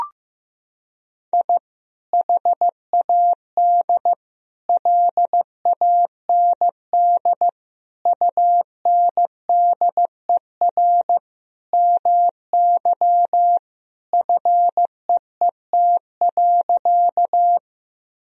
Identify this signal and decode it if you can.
Morse code, heard as a keyed tone, about 15 wpm, 700 Hz. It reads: I HAD LAND UNDER MY FEET.
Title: Sentences from Top 1000 Words - Review Part 1 - 15wpm